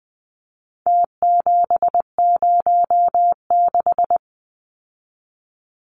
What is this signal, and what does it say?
Morse code, heard as a keyed tone, about 20 wpm, 700 Hz. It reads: T706
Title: Numbers - Contest Style - 4-Digit 20wpm